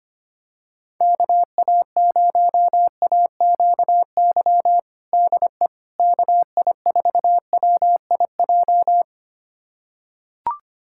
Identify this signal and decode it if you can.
Morse code, heard as a keyed tone, about 25 wpm, 700 Hz. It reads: KA0AQY DE KI4WIJ
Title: USA Callsign DE Callsign - 6 Characters - 25wpm